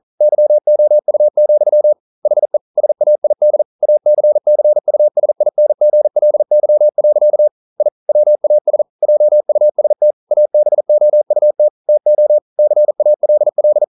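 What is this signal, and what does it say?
Morse code, heard as a keyed tone, about 39 wpm, 600 Hz. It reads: YOU, HE SAID ACCUSINGLY. I WAS JUST ABOUT TO CALL
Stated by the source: Morse code practice at 41 wpm, from CW med Gnister.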